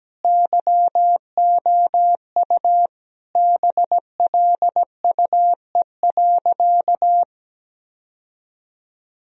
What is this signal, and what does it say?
Morse code, heard as a keyed tone, about 17 wpm, 700 Hz. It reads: YOU BLUE.
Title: Sentences from Top 1000 Words - Review Part 3 - 17wpm